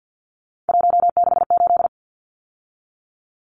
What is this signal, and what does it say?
Morse code, heard as a keyed tone, about 50 wpm, 700 Hz. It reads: 168
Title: Numbers - 3-Digit 50wpm